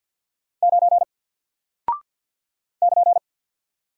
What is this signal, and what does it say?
Morse code, heard as a keyed tone, about 48 wpm, 700 Hz. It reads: 9 <KN>
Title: Single Character (Letter:Number:Punctuation:Prosign) Alt 50wpm